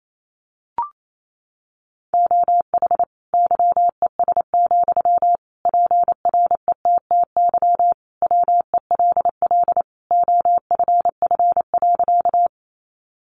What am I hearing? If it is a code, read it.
Morse code, heard as a keyed tone, about 28 wpm, 700 Hz. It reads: OH YES, PRETTY WELL OFF.
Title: Sentences from Top 1000 Words 28wpm